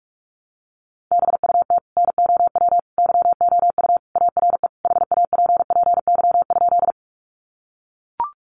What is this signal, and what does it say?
Morse code, heard as a keyed tone, about 44 wpm, 700 Hz. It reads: BUT NOW YOU ARE HAPPY?
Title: Sentences from Top 1000 Words 45wpm